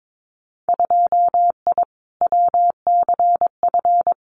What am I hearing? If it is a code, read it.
Morse code, heard as a keyed tone, about 22 wpm, 700 Hz. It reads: 2I WCF
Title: Winter Field Day Exchanges 22wpm